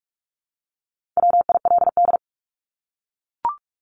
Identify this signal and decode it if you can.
Morse code, heard as a keyed tone, about 44 wpm, 700 Hz. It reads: WILD
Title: Top 1501-1750 Words 45wpm